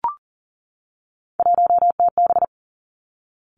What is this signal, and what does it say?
Morse code, heard as a keyed tone, about 40 wpm, 700 Hz. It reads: 1TB
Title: Three Challenging Letter-Number Combinations 40wpm